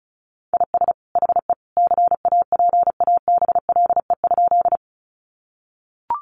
Morse code, heard as a keyed tone, about 34 wpm, 700 Hz. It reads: IS HE CAPABLE?
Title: Sentences from Top 1200 Words 35wpm